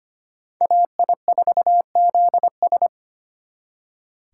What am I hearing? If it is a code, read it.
Morse code, heard as a keyed tone, about 29 wpm, 700 Hz. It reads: AI4ZS
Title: Speed Racing - USA 2x2 Call Signs 25wpm